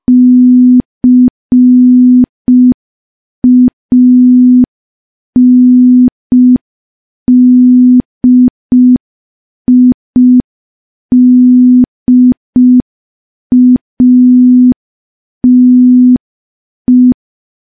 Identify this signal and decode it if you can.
Morse code, heard as a keyed tone, about 5 wpm, 250 Hz. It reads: CANDIDATE
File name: morse / morse-training-05WPM